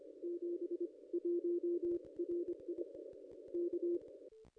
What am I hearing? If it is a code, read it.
Morse code, heard as a keyed tone, about 25 wpm, spaced as for 21 wpm, 350 Hz. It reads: 71 RI K
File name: contests / ss-cw-k5tr-op-n5rz -0200